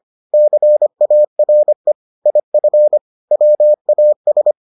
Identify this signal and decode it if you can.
Morse code, heard as a keyed tone, about 25 wpm, 600 Hz. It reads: CARE IF WAS